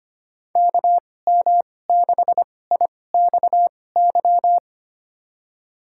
Morse code, heard as a keyed tone, about 25 wpm, spaced as for 18 wpm, 700 Hz. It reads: KM6IXY
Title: USA Call Signs - 2x Character Spacing - 25wpm